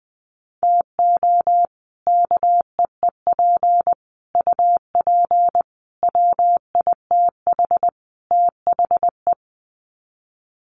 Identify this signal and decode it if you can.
Morse code, heard as a keyed tone, about 20 wpm, 700 Hz. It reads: TO KEEP UP WITH THE
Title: Sets of 5 Words Encore 20wpm